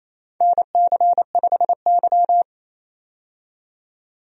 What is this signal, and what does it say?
Morse code, heard as a keyed tone, about 28 wpm, 700 Hz. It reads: NC5Y
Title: USA Call Signs - 4 Characters - 28wpm